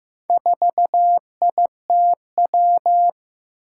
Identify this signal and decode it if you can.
Morse code, heard as a keyed tone, about 15 wpm, 700 Hz. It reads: 4ITW
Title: Four Challenging Letter-Number Combinations 15wpm